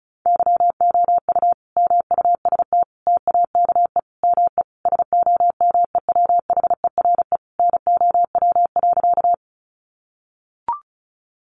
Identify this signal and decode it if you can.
Morse code, heard as a keyed tone, about 34 wpm, 700 Hz. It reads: YOU MUST TAKE ME SOMEWHERE NOW.
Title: Sentences from Top 1400 Words 35wpm